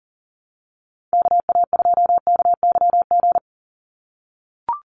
Morse code, heard as a keyed tone, about 40 wpm, 700 Hz. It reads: KA2KYG
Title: USA Call Signs 40wpm